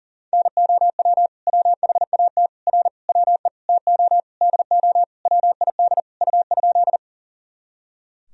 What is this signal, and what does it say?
Morse code, heard as a keyed tone, about 40 wpm, 700 Hz. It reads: NOW WHAT R WE TO DO WID U?
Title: Sentences from Top 100 Words and Abbreviations 40wpm